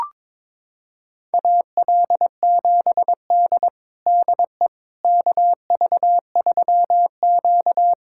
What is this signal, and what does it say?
Morse code, heard as a keyed tone, about 22 wpm, 700 Hz. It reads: AL7D DE KV3Q
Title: USA Callsign DE Callsign - 4 Characters - 22wpm